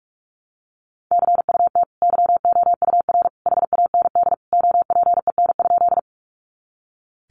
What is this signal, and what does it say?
Morse code, heard as a keyed tone, about 44 wpm, 700 Hz. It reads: CUT YOUR HAND OPEN?
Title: Sentences from Top 300 Words 45wpm